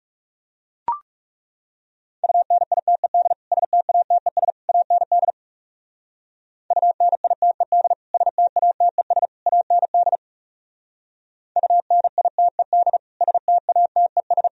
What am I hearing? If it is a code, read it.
Morse code, heard as a keyed tone, about 37 wpm, 700 Hz. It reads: UNITED STATES AND UNITED STATES AND UNITED STATES
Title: Speed-Racing - Sets of 3 Words 30wpm